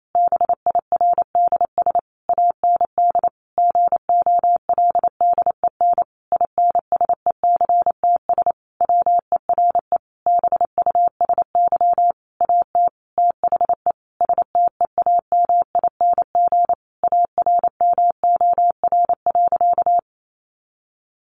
Morse code, heard as a keyed tone, about 28 wpm, 700 Hz. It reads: BIRDS AND GOLDEN INSECTS WERE BUSY AT THE STEAMING ARMOR.
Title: A Sound Of Thunder 28wpm